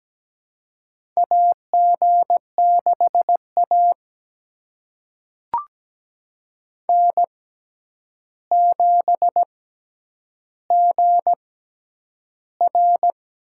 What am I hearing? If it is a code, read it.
Morse code, heard as a keyed tone, about 17 wpm, spaced as for 9 wpm, 700 Hz. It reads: AG6A N7GR
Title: USA Call Signs - 4 Characters - 6x Character Spacing - 17wpm